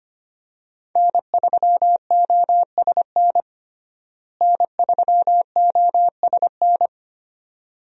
Morse code, heard as a keyed tone, about 25 wpm, 700 Hz. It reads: N3OSN N3OSN
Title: USA Call Signs - 5 Characters - Repeated 2x - 25wpm